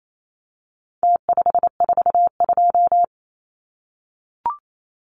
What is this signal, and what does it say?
Morse code, heard as a keyed tone, about 28 wpm, 700 Hz. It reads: T542
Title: Numbers - Contest Style - 4-Digit 28wpm